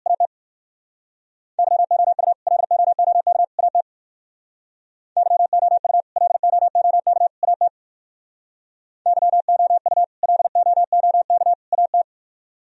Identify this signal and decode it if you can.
Morse code, heard as a keyed tone, about 57 wpm, 700 Hz. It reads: AT YOU LOOK AT YOU LOOK AT YOU LOOK AT